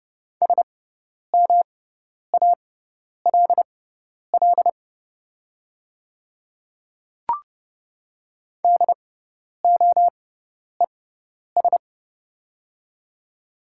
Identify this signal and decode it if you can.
Morse code, heard as a keyed tone, about 30 wpm, spaced as for 10 wpm, 700 Hz. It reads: SMALL DOES
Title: 6x Character Spacing - Top 200 Words - 30wpm